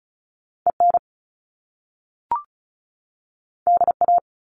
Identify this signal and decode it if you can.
Morse code, heard as a keyed tone, about 36 wpm, 700 Hz. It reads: EN DA
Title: Top Two Letter Combinations in Words 35wpm